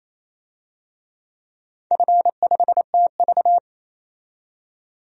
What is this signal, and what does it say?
Morse code, heard as a keyed tone, about 28 wpm, 700 Hz. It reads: F5TV
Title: DXCC-100 Call Signs - 4 Characters - 28wpm